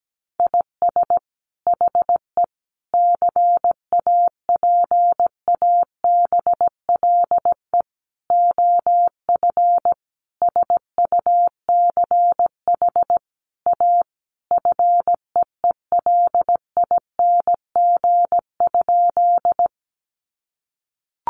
Morse code, heard as a keyed tone, about 17 wpm, 700 Hz. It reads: IS HE CAPABLE OF SUCH A FEELING?
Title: Sentences from Top 1200 Words 17wpm